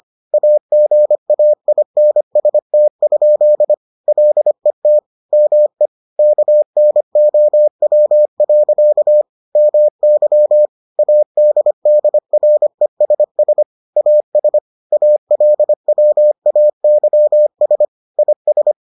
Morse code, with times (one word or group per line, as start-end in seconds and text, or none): 0.3-3.7 AGAINST?
4.1-5.0 LET
5.3-5.9 ME
6.2-9.2 KNOW.
9.6-10.7 MY
11.0-13.6 ADDRESS
14.0-14.6 AS
14.9-17.9 ALWAYS
18.2-18.7 IS